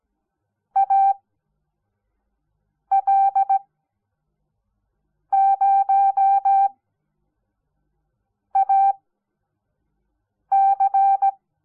Morse code, heard as a keyed tone, about 17 wpm, spaced as for 4 wpm, 780 Hz. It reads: AL0AC
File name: Onlinelehrgang / morsen / DJ4UF Morsen06